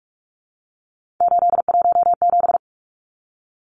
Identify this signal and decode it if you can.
Morse code, heard as a keyed tone, about 44 wpm, 700 Hz. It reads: 817
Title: Numbers - 3-Digit 45wpm